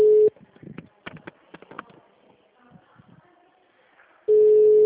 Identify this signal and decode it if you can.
Ringback tone (Europe, 425 Hz).
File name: out- -2002- - - .3355